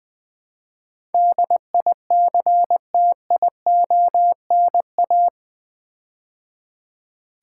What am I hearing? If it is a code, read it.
Morse code, heard as a keyed tone, about 20 wpm, 700 Hz. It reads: DICTIONA
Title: Build A Word - 4 Letters At A Time - 10-Letter Words - 20wpm